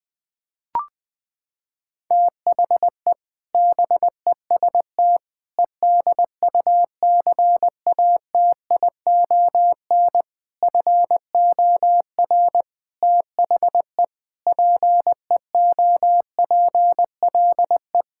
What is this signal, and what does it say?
Morse code, heard as a keyed tone, about 20 wpm, 700 Hz. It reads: THE BEST EDUCATION FOR THE PEOPLE
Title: Sentences from Top 1000 Words - Review Part 6 - 20wpm